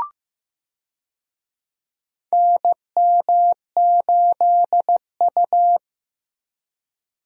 Morse code, heard as a keyed tone, about 15 wpm, 700 Hz. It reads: NM8U
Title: USA Call Signs - 4 Characters - Repeated 2x - 15wpm